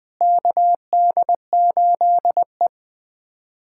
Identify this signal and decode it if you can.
Morse code, heard as a keyed tone, about 21 wpm, 700 Hz. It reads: KD8E
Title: USA Callsign DE Callsign - 4 Characters - 20wpm